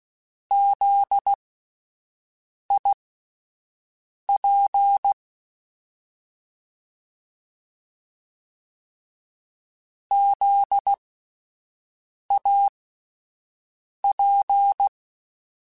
Morse code, heard as a keyed tone, about 16 wpm, spaced as for 5 wpm, 800 Hz. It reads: ZIP ZAP
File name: Spaced Alphas / Spaced Alphas mp3s / Zs sp @10